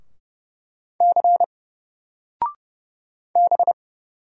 Morse code, heard as a keyed tone, about 31 wpm, 700 Hz. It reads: C B
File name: Single-Letter-Number-30wpm